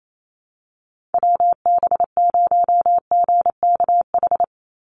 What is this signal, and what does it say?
Morse code, heard as a keyed tone, about 28 wpm, 700 Hz. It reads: WB0GKH
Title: Call Signs Repeated 3x - 28wpm